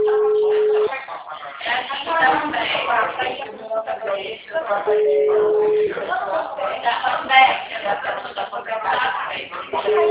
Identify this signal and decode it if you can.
Ringback tone (Europe, 425 Hz), with speech.